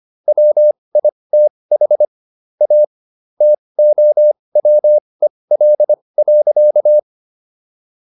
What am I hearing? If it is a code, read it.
Morse code, heard as a keyed tone, about 25 wpm, spaced as for 20 wpm, 600 Hz. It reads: WITH A TOWEL.